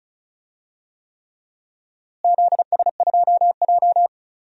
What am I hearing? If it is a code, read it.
Morse code, heard as a keyed tone, about 34 wpm, 700 Hz. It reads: ZS2J
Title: DXCC-100 Call Signs - 4 Characters - Repeated 3x - 35wpm